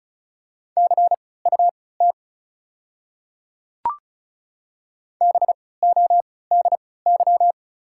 Morse code, heard as a keyed tone, about 36 wpm, spaced as for 20 wpm, 690 Hz. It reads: CUT BODY
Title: 3x Character Spacing - Top 500 Words - 35wpm